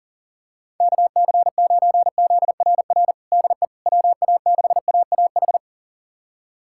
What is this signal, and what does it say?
Morse code, heard as a keyed tone, about 40 wpm, 700 Hz. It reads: KC9ZRR DE WA6AAH
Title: USA Callsign DE Callsign 40wpm